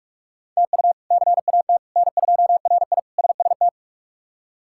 Morse code, heard as a keyed tone, about 42 wpm, 700 Hz. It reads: TU CAT N2RI SST
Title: K1USN SST QSO Practice - 45wpm